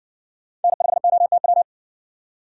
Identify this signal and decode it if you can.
Morse code, heard as a keyed tone, about 60 wpm, 700 Hz. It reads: N5OTW